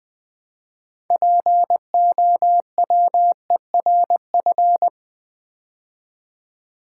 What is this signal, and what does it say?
Morse code, heard as a keyed tone, about 20 wpm, 700 Hz. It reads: POWERF
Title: Build A Word - 3 Letters At A Time - 8-Letter Words - 20wpm